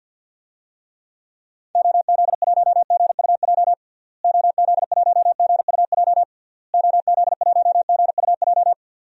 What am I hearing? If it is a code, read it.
Morse code, heard as a keyed tone, about 51 wpm, 700 Hz. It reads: OZ1GUJ OZ1GUJ OZ1GUJ
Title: DXCC-100 Call Signs - 6 Characters - Repeated 3x - 50wpm